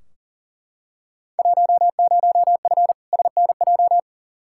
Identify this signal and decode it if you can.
Morse code, heard as a keyed tone, about 40 wpm, 700 Hz. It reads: 10F SNJ
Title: Field Day Exchanges Encore - 40wpm